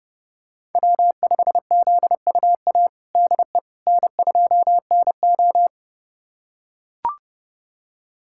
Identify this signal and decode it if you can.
Morse code, heard as a keyed tone, about 30 wpm, 700 Hz. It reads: W5ZUA DE N2NO
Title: USA Callsign DE Callsign 30wpm